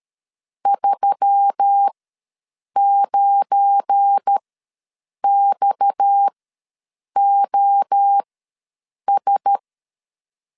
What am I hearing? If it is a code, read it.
Morse code, heard as a keyed tone, about 13 wpm, spaced as for 7 wpm, 800 Hz. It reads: 39XOS
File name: LEZIONI / 7 GF9Z / 7 GF9 ws 12wpm 3344